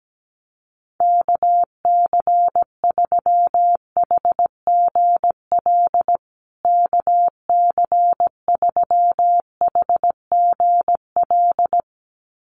Morse code, heard as a keyed tone, about 17 wpm, 700 Hz. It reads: KC3HGL KC3HGL
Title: USA Call Signs Repeated 2x 17wpm